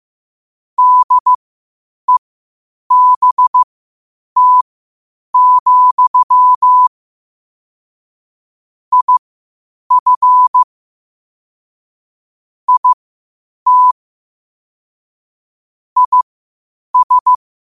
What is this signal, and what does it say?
Morse code, heard as a keyed tone, about 15 wpm, spaced as for 8 wpm, 1000 Hz. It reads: DEBT, IF IT IS
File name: qotdfiles / 8wpm